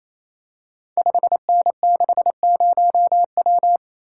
Morse code, heard as a keyed tone, about 28 wpm, 700 Hz. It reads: 5N60W
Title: Code Groups - 5 Letter:Numbers - 28wpm